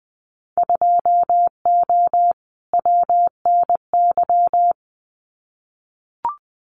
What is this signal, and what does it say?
Morse code, heard as a keyed tone, about 20 wpm, 700 Hz. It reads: 2O WNY
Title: Winter Field Day Exchanges Encore 20wpm